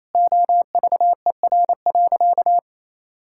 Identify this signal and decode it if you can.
Morse code, heard as a keyed tone, about 28 wpm, 700 Hz. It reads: OVER.
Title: Sentences from Top 1600 Words 28wpm